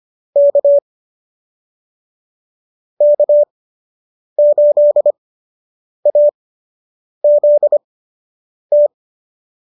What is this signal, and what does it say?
Morse code, heard as a keyed tone, about 25 wpm, spaced as for 8 wpm, 590 Hz. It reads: K K8AZT
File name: qso006 08